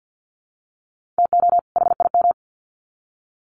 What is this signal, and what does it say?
Morse code, heard as a keyed tone, about 50 wpm, 700 Hz. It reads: TO HIM
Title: Sets of 2 Words 50wpm